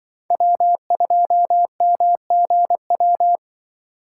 Morse code, heard as a keyed tone, about 25 wpm, 700 Hz. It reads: W2MGW
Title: Speed Racing - USA 1x3 Call Signs 20wpm